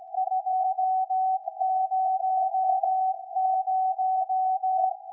Morse code, heard as a keyed tone, about 16 wpm, 750 Hz. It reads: 200
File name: iz7slz / VLF / SAQ 13-02-2023